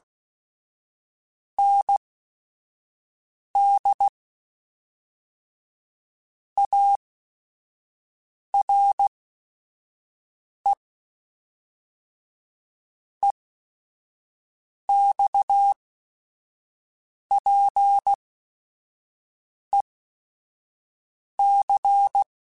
Morse code, heard as a keyed tone, about 16 wpm, spaced as for 5 wpm, 800 Hz. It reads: ND ARE EXPEC
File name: QSOs / QSOs-mp3s / WA7QOX 1@5